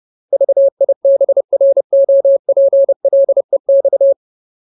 Morse code, heard as a keyed tone, about 30 wpm, 550 Hz. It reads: VIBROPLEX
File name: Ultimate Word List / 30WPM Ultimate